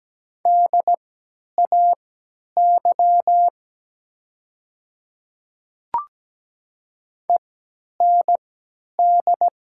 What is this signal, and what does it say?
Morse code, heard as a keyed tone, about 17 wpm, spaced as for 10 wpm, 700 Hz. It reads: DAY END
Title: 3x Character Spacing - Top 300 Words - 17wpm